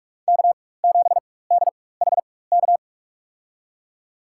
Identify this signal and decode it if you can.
Morse code, heard as a keyed tone, about 43 wpm, spaced as for 21 wpm, 700 Hz. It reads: K7DHK
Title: USA Call Signs - 5 Characters - 4x Character Spacing - 45wpm